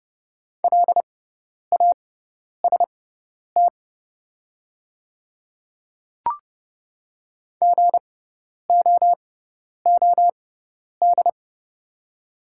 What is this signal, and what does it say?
Morse code, heard as a keyed tone, about 30 wpm, spaced as for 10 wpm, 700 Hz. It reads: LAST GOOD
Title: 6x Character Spacing - Top 300 Words - 30wpm